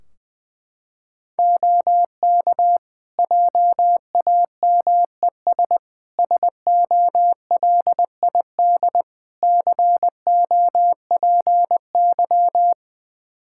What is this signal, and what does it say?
Morse code, heard as a keyed tone, about 21 wpm, 700 Hz. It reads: OK JAMES SOLID COPY
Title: Practice QSO - Exchange 3 - More Info - 20wpm